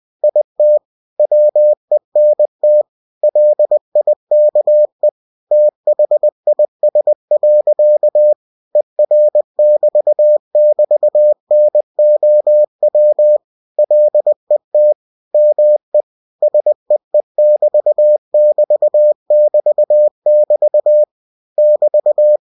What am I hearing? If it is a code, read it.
Morse code, heard as a keyed tone, about 20 wpm, 600 Hz. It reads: IT WENT LIKE THIS. ER<BT><BT>NOW LET ME SEE<BT><BT><BT><BT> <BT>